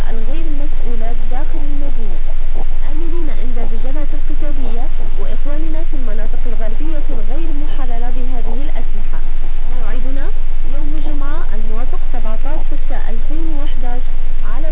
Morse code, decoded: none